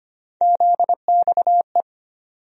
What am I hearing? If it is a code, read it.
Morse code, heard as a keyed tone, about 25 wpm, 700 Hz. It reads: ZXE
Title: USA Call Sign Postfixes - 3 Letters 25wpm